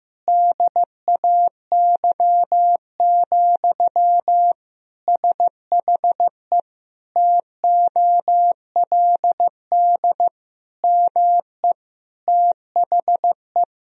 Morse code, heard as a keyed tone, about 15 wpm, 700 Hz. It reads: DAY, SHE TOLD ME THE
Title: Sentences from Top 1000 Words - Review Part 4 - 15wpm